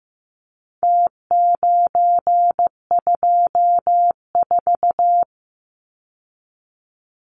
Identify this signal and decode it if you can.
Morse code, heard as a keyed tone, about 15 wpm, 700 Hz. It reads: T924
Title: Numbers - Contest Style - 4-Digit 15wpm